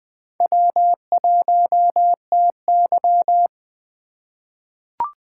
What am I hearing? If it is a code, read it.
Morse code, heard as a keyed tone, about 20 wpm, 700 Hz. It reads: W1TY
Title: USA Call Signs - 4 Characters - 3x Character Spacing - 20wpm